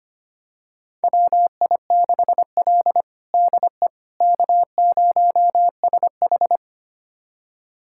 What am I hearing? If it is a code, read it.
Morse code, heard as a keyed tone, about 25 wpm, 700 Hz. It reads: WI6L DE K0SH
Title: USA Callsign DE Callsign - 4 Characters - 25wpm